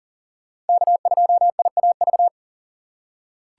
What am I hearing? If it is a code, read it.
Morse code, heard as a keyed tone, about 40 wpm, 700 Hz. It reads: K2IAV
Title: USA Call Signs - 5 Characters - 3x Character Spacing - 40wpm